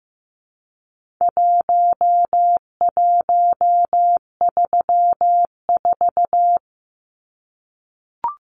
Morse code, heard as a keyed tone, about 15 wpm, 700 Hz. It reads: 1134
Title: Numbers - Contest Style - 4-Digit 15wpm